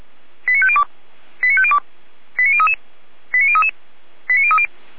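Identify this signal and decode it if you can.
Five-tone selective call (ZVEI), heard as quick runs of five tones, tones of about 70 ms. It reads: 89502 89502 89033 89033 89030
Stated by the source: ZVEI five-tone selective call received by SDR, on the feuerwehr-orlamuende fire brigade audio server.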